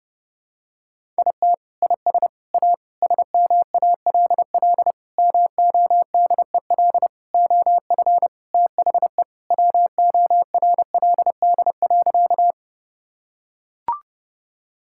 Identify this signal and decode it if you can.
Morse code, heard as a keyed tone, about 30 wpm, 700 Hz. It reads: IT IS A SMALL MODEL OF THE WORLD.
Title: Sentences from Top 1000 Words - Review Part 5 - 30wpm